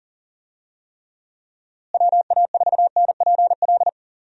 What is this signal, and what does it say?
Morse code, heard as a keyed tone, about 37 wpm, 700 Hz. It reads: WA4NPL
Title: USA Call Signs - 6 Characters - Repeated 3x - 40wpm